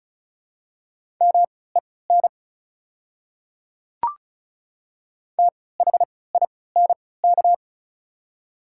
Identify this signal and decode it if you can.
Morse code, heard as a keyed tone, about 34 wpm, spaced as for 20 wpm, 700 Hz. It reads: MEN THINK
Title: 3x Character Spacing - Top 200 Words - 35wpm